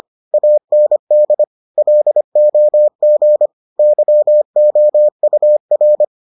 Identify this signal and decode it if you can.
Morse code, heard as a keyed tone, about 25 wpm, 600 Hz. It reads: AND LOG YOUR